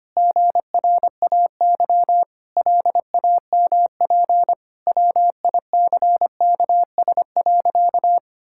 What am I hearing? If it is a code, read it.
Morse code, heard as a keyed tone, about 25 wpm, 700 Hz. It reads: GRAY LAMP WICKS.
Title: Pangram Challenge v2 - 25wpm